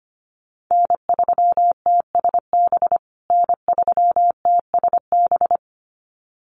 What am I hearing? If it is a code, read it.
Morse code, heard as a keyed tone, about 25 wpm, 700 Hz. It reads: N3TSB N3TSB
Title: USA Call Signs Repeated 2x 25wpm